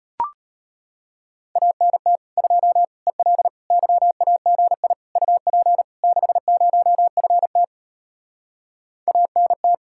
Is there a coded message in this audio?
Yes: Morse code from 1.6 to 9.7 s, heard as a keyed tone, about 33 wpm, 700 Hz. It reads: ANT 2 EL YAGI UP 60FT ANT